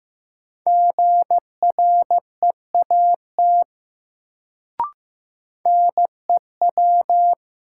Morse code, heard as a keyed tone, about 15 wpm, 700 Hz. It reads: GREAT NEW
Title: Sets of 2 Words - New Word Drill - 15wpm